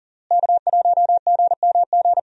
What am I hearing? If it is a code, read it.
Morse code, heard as a keyed tone, about 37 wpm, 700 Hz. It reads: K1GMG
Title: USA Call Signs - 5 Characters - Repeated 2x - 40wpm